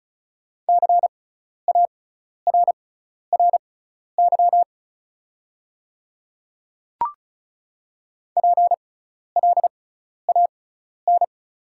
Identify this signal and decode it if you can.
Morse code, heard as a keyed tone, about 34 wpm, spaced as for 12 wpm, 700 Hz. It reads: CARRY PLAN
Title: 6x Character Spacing - Top 500 Words - 35wpm